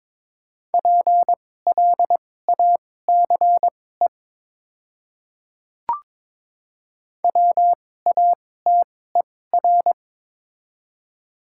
Morse code, heard as a keyed tone, about 21 wpm, spaced as for 16 wpm, 700 Hz. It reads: PLACE WATER
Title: 2x Character Spacing - Top 200 Words - 22wpm